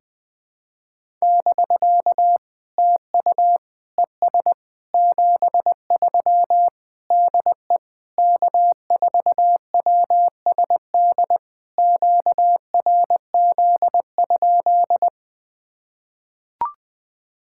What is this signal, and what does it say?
Morse code, heard as a keyed tone, about 20 wpm, 690 Hz. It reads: <BK> TU ES 73 DE K4WSD QRZ?
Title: POTA QSO Practice - 20wpm